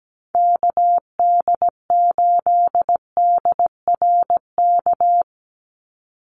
Morse code, heard as a keyed tone, about 17 wpm, 700 Hz. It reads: KD8DRK
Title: USA Call Signs 17wpm